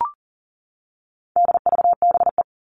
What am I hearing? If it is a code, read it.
Morse code, heard as a keyed tone, about 40 wpm, 700 Hz. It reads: DVBE
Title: All Cut Numbers - 4-Digit 40wpm